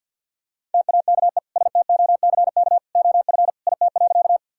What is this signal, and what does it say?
Morse code, heard as a keyed tone, about 50 wpm, 700 Hz. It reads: TAKE STOCK OF IT.
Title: Sentences from Top 1000 Words - Review Part 5 - 50wpm